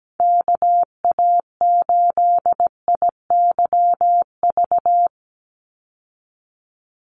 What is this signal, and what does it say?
Morse code, heard as a keyed tone, about 17 wpm, 700 Hz. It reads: KA8IYV
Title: USA Call Signs 17wpm